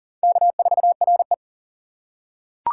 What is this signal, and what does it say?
Morse code, heard as a keyed tone, about 40 wpm, 700 Hz. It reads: K4RE
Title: USA Call Signs - 4 Characters - 3x Character Spacing - 40wpm